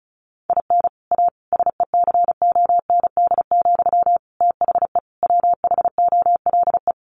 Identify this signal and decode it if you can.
Morse code, heard as a keyed tone, about 34 wpm, 700 Hz. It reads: IN A SECOND, THE WHOLE
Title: Sentences from Top 1500 Words 35wpm